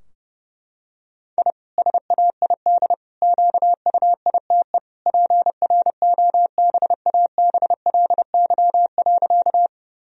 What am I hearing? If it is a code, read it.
Morse code, heard as a keyed tone, about 30 wpm, 700 Hz. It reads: I SAID QUITE PROBABLY.
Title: Sentences from Top 1000 Words - Review Part 4 - 30wpm